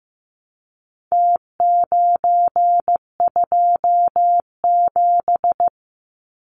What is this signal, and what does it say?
Morse code, heard as a keyed tone, about 15 wpm, 700 Hz. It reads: T927
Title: Numbers - Contest Style - 4-Digit 15wpm